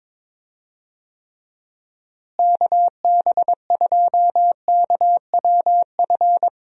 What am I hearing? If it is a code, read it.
Morse code, heard as a keyed tone, about 22 wpm, 700 Hz. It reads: KB2KWF
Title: USA Call Signs - 6 Characters - Repeated 2x - 22wpm